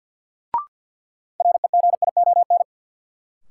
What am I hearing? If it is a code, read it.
Morse code, heard as a keyed tone, about 53 wpm, 700 Hz. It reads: REGION
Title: Sentences from Top 1500 Words - New Word Drill - 50wpm